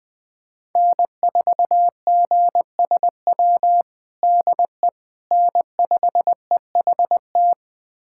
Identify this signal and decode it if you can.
Morse code, heard as a keyed tone, about 20 wpm, 700 Hz. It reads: N4GSW DE N5EHT
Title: USA Callsign DE Callsign - 5 Characters - 20wpm